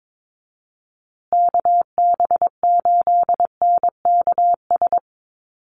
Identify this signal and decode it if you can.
Morse code, heard as a keyed tone, about 22 wpm, 690 Hz. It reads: KB8NKS
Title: USA Call Signs Repeated 3x 22wpm